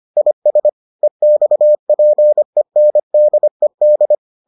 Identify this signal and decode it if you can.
Morse code, heard as a keyed tone, about 25 wpm, 600 Hz. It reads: IS EXPENDED